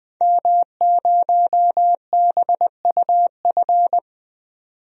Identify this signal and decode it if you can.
Morse code, heard as a keyed tone, about 20 wpm, 700 Hz. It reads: M0BUF
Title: DXCC-100 Call Signs - 5 Characters - 20wpm